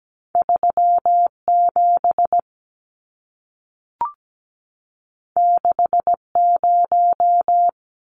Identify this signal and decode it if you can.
Morse code, heard as a keyed tone, about 17 wpm, 700 Hz. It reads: 37 60
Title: Numbers - 2-Digit 17wpm